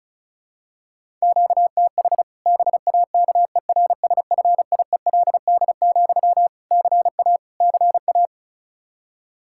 Morse code, heard as a keyed tone, about 36 wpm, 700 Hz. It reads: QTH BAKERSFIELD, CA CA
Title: Practice QSO - Exchange 1 - Beginning - 35wpm